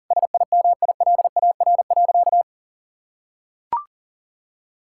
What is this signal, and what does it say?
Morse code, heard as a keyed tone, about 40 wpm, 700 Hz. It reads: SIMILAR.
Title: Sentences from Top 1000 Words - Review Part 5 - 40wpm